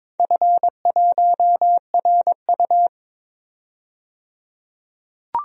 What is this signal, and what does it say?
Morse code, heard as a keyed tone, about 22 wpm, 700 Hz. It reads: F1RU
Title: DXCC-100 Call Signs - 4 Characters - 22wpm